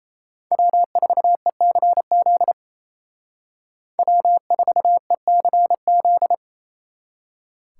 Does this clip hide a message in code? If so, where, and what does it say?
Yes: Morse code from 0.5 to 6.3 s, heard as a keyed tone, about 36 wpm, 700 Hz. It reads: W4ECZ W4ECZ